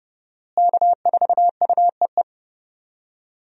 Morse code, heard as a keyed tone, about 30 wpm, 700 Hz. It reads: K4UEE
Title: USA Call Signs - 5 Characters - Repeated 3x - 30wpm